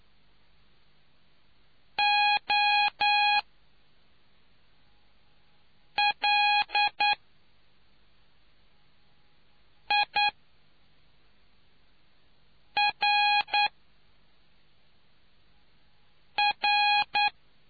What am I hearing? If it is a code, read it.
Morse code, heard as a keyed tone, about 9 wpm, spaced as for 3 wpm, 830 Hz. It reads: OLIRR